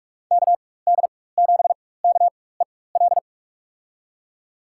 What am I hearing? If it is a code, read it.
Morse code, heard as a keyed tone, about 45 wpm, spaced as for 21 wpm, 700 Hz. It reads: KD7KEL
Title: USA Call Signs - 6 Characters - 4x Character Spacing - 45wpm